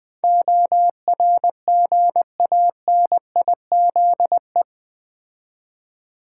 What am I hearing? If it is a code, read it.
Morse code, heard as a keyed tone, about 20 wpm, 700 Hz. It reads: ORGANIZE
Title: Build A Word - 3 Letters At A Time - 8-Letter Words - 20wpm